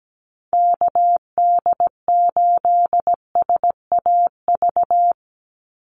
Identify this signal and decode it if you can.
Morse code, heard as a keyed tone, about 17 wpm, 700 Hz. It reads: KD8SAV
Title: Call Signs - 17wpm